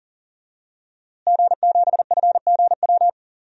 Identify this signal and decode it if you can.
Morse code, heard as a keyed tone, about 37 wpm, 700 Hz. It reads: G7FGW
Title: DXCC-100 Call Signs - 5 Characters - Repeated 2x - 40wpm